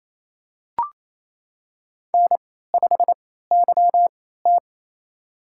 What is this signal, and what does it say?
Morse code, heard as a keyed tone, about 28 wpm, spaced as for 16 wpm, 700 Hz. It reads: N5YT
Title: USA Call Signs - 4 Characters - 3x Character Spacing - 28wpm